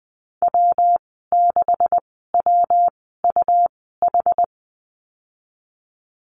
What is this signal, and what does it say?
Morse code, heard as a keyed tone, about 20 wpm, spaced as for 15 wpm, 700 Hz. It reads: W6WUH
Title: 193 Lesson 31g - Letter v - Call Signs 20wpm